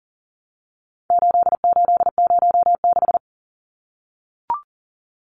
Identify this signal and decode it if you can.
Morse code, heard as a keyed tone, about 40 wpm, 700 Hz. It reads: 8806
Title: Numbers - 4-Digit 40wpm